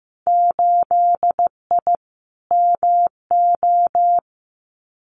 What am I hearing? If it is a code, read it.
Morse code, heard as a keyed tone, about 15 wpm, 700 Hz. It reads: 8I MO
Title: Winter Field Day Exchanges Encore 15wpm